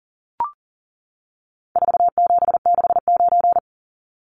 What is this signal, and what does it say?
Morse code, heard as a keyed tone, about 40 wpm, 700 Hz. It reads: 4769
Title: Numbers - 4-Digit 40wpm